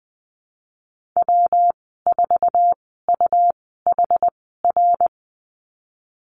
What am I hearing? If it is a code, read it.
Morse code, heard as a keyed tone, about 20 wpm, spaced as for 15 wpm, 700 Hz. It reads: W4UHR
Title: 193 Lesson 31g - Letter v - Call Signs 20wpm